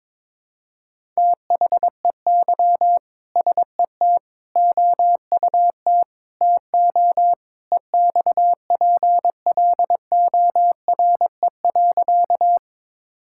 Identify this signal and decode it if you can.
Morse code, heard as a keyed tone, about 21 wpm, 700 Hz. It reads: THEY SET OUT TO EXPLORE.
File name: practice / 22wpm / Sentences with 2-Word Phrases - 22wpm